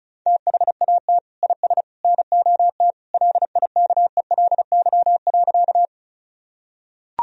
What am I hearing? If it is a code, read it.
Morse code, heard as a keyed tone, about 34 wpm, 700 Hz. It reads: THAT IS NOT LIKELY.
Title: Sentences from Top 1000 Words - Review Part 4 - 35wpm